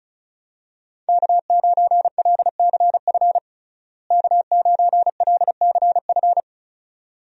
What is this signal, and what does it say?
Morse code, heard as a keyed tone, about 36 wpm, 700 Hz. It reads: K9LCF K9LCF
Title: USA Call Signs - 5 Characters - Repeated 2x - 35wpm